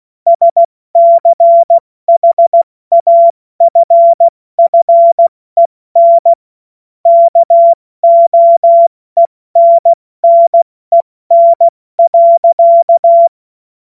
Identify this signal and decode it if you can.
Morse code, heard as a keyed tone, about 16 wpm, 680 Hz. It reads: SCHAFFEN KOENNEN.
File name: Morsekurs / Witze / Witz 21.November 2023 70 F80